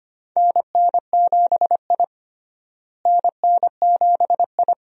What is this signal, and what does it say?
Morse code, heard as a keyed tone, about 25 wpm, 700 Hz. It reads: NN7I NN7I
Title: USA Call Signs - 4 Characters - Repeated 2x - 25wpm